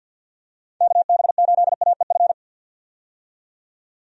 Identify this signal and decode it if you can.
Morse code, heard as a keyed tone, about 50 wpm, 700 Hz. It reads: KB8AEF
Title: USA Call Signs - 6 Characters - 50wpm